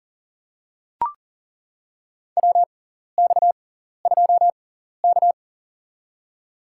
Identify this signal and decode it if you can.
Morse code, heard as a keyed tone, about 40 wpm, spaced as for 14 wpm, 700 Hz. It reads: WX2K
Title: USA Call Signs - 6x Character Spacing - 40wpm